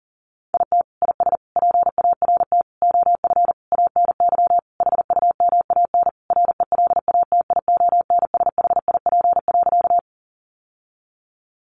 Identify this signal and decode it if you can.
Morse code, heard as a keyed tone, about 40 wpm, 700 Hz. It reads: IT IS PART OF ANY HUMAN RELATIONSHIP.
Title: Sentences from Top 1100 Words 40wpm